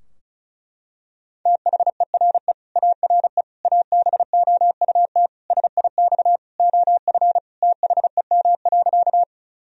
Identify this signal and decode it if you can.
Morse code, heard as a keyed tone, about 34 wpm, 700 Hz. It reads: THERE ARE ABOUT SIX OF THEM.
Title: Sentences from Top 1000 Words - Review Part 2 - 35wpm